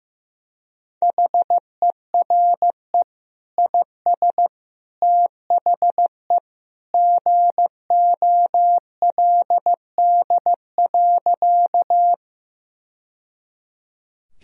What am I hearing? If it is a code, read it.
Morse code, heard as a keyed tone, about 15 wpm, 700 Hz. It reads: HERE IS THE GOLD.
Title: Sentences from Top 1000 Words - Review Part 3 - 15wpm